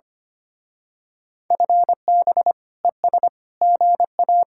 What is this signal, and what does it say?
Morse code, heard as a keyed tone, about 29 wpm, 700 Hz. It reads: FB ES GA
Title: Speed-Racing Common QSO Phrases Encore - 25wpm